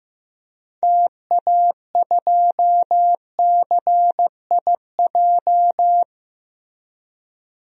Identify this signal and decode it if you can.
Morse code, heard as a keyed tone, about 16 wpm, 700 Hz. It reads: TA2CIJ
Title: DXCC-100 Call Signs - 6 Characters - Repeated 3x - 15wpm